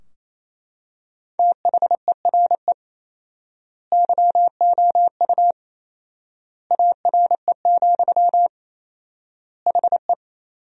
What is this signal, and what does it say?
Morse code, heard as a keyed tone, about 28 wpm, 700 Hz. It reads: THERE YOU ARE, HE
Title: 4x Word-Spacing - Sentences from Top 100 Words - 28wpm